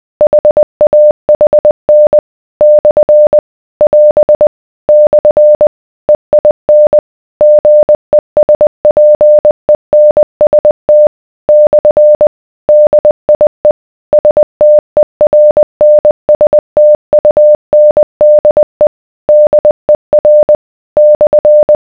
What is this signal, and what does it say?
Morse code, heard as a keyed tone, about 20 wpm, 600 Hz. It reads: HAHN / <AS> / EIN GESPENST / DIE STERNSTUNDE DER /